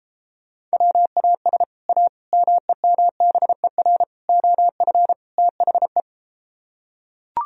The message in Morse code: WAS A MEMBER OF THE

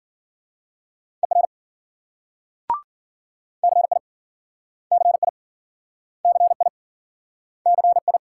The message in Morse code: ER CI CI CI CI

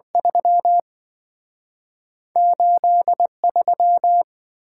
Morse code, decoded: 3 83

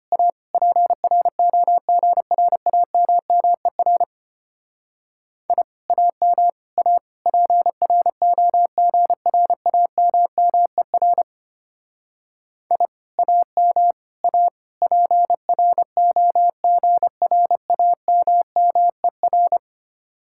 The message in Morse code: A PROGRAMMER I AM A PROGRAMMER I AM A PROGRAMMER

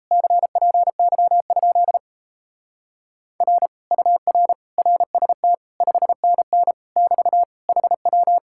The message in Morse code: CPY? R UR RST 5NN <BT> HW

